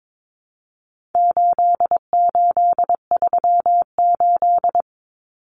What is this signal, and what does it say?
Morse code, heard as a keyed tone, about 22 wpm, 700 Hz. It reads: 8838
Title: Numbers - 4-Digit 22wpm